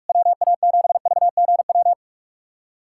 Morse code, heard as a keyed tone, about 46 wpm, 690 Hz. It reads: WA7VGW
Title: SOTA QSO Practice 45wpm